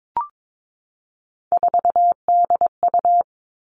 Morse code, heard as a keyed tone, about 22 wpm, 700 Hz. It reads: 4DU
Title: All Cut Numbers - 3-Digit 22wpm